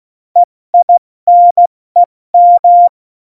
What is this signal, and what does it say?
Morse code, heard as a keyed tone, about 16 wpm, 700 Hz. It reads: EINEM
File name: Morsekurs / Witze / Witz 29. Dezember 2020, 70 F80